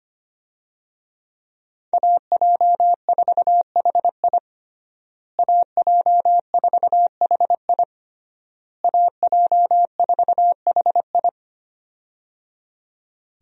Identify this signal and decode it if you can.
Morse code, heard as a keyed tone, about 25 wpm, 700 Hz. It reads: AJ4HI AJ4HI AJ4HI
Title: USA Call Signs - 5 Characters - Repeated 3x - 25wpm